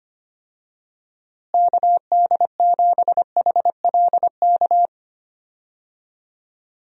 Morse code, heard as a keyed tone, about 25 wpm, 700 Hz. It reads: KD7HLK
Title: USA Call Signs - 6 Characters - Repeated 2x - 25wpm